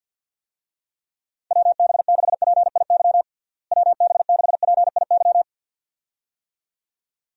Morse code, heard as a keyed tone, about 46 wpm, 700 Hz. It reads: WB6PIY WB6PIY
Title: USA Call Signs - 6 Characters - Repeated 2x - 50wpm